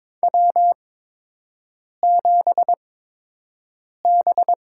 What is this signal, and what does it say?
Morse code, heard as a keyed tone, about 22 wpm, spaced as for 6 wpm, 700 Hz. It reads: W7B
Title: USA Call Signs - 8x Character Spacing - 22wpm